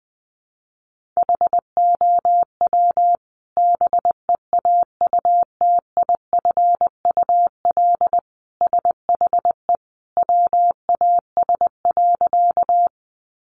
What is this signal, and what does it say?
Morse code, heard as a keyed tone, about 20 wpm, 700 Hz. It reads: HOW BEAUTIFUL SHE WAS.
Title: Sentences from Top 900 Words 20wpm